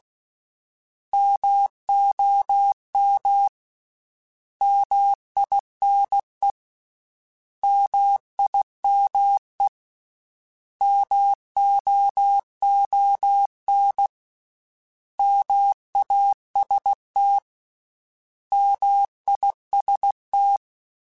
Morse code, heard as a keyed tone, about 16 wpm, 800 Hz. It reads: MOM MINE MIME MOON MAST MIST
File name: Misc / Misc-mp3s / Misc @15